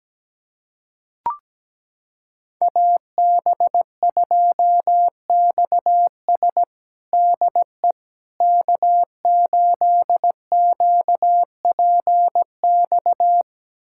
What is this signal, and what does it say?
Morse code, heard as a keyed tone, about 17 wpm, 700 Hz. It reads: AB2XS DE K8QPX
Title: USA Callsign DE Callsign - 5 Characters - 17wpm